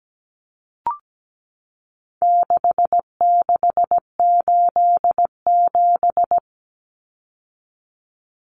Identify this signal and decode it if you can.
Morse code, heard as a keyed tone, about 17 wpm, 700 Hz. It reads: 6687
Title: Common Cut Numbers - 4-Digit 17wpm